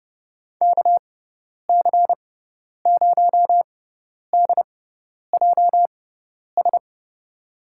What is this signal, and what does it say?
Morse code, heard as a keyed tone, about 30 wpm, spaced as for 10 wpm, 700 Hz. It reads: KC0DJS
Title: USA Call Signs - 6 Characters - 6x Character Spacing - 30wpm